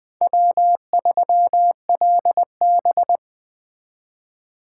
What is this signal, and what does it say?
Morse code, heard as a keyed tone, about 24 wpm, 700 Hz. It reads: W3LB
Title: Speed Racing - USA 1x2 Call Signs 20wpm